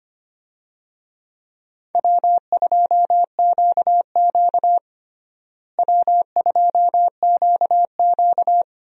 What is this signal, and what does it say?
Morse code, heard as a keyed tone, about 25 wpm, 700 Hz. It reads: W2QQ W2QQ
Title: USA Call Signs - 4 Characters - Repeated 3x - 25wpm